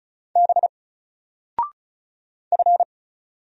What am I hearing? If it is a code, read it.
Morse code, heard as a keyed tone, about 33 wpm, 700 Hz. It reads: B F
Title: Single Letter-Number 35wpm